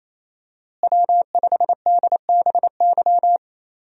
Morse code, heard as a keyed tone, about 28 wpm, 700 Hz. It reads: W5DBY
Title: USA Call Signs - 5 Characters - Repeated 2x - 28wpm